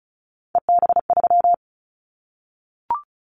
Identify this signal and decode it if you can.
Morse code, heard as a keyed tone, about 34 wpm, 700 Hz. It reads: EB3
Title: Callsign Prefixes for Top 25 DXCC - Repeated 3x - 35wpm